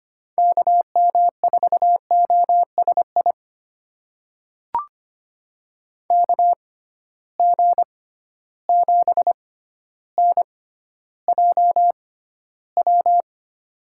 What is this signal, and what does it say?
Morse code, heard as a keyed tone, about 25 wpm, spaced as for 13 wpm, 700 Hz. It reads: KM4OSI KG7NJW
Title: USA Call Signs - 6 Characters - 6x Character Spacing - 25wpm